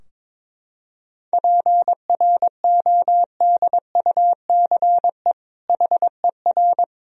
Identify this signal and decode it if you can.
Morse code, heard as a keyed tone, about 22 wpm, 700 Hz. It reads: PRODUCE HER
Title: Sentences from Top 1000 Words - Review Part 3 - 22wpm